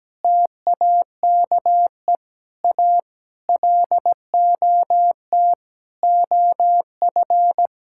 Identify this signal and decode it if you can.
Morse code, heard as a keyed tone, about 17 wpm, 700 Hz. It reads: TAKE A LOT OF
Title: Sentences from Top 1000 Words - Review Part 6 - 17wpm